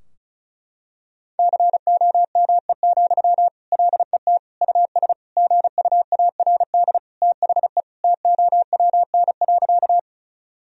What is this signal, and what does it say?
Morse code, heard as a keyed tone, about 36 wpm, 700 Hz. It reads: COME, LET US GUARD THE TOWN.
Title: Sentences from Top 1000 Words - Review Part 4 - 35wpm